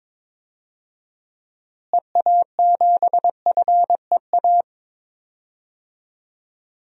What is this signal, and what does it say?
Morse code, heard as a keyed tone, about 22 wpm, 700 Hz. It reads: EA7FEA
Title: DXCC-100 Call Signs - 6 Characters - Repeated 3x - 22wpm